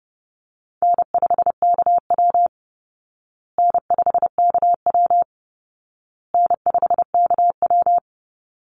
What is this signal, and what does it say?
Morse code, heard as a keyed tone, about 30 wpm, 700 Hz. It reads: N5KW N5KW N5KW
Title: Easy Call Signs Repeated 3x - 30wpm